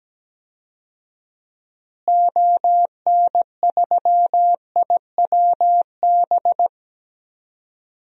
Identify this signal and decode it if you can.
Morse code, heard as a keyed tone, about 17 wpm, 700 Hz. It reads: ON3IWB
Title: DXCC-100 Call Signs - 6 Characters - Repeated 3x - 17wpm